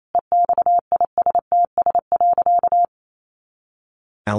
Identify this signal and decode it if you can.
Morse code, heard as a keyed tone, about 28 wpm, 700 Hz. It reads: EXISTS.
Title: A Sound Of Thunder 28wpm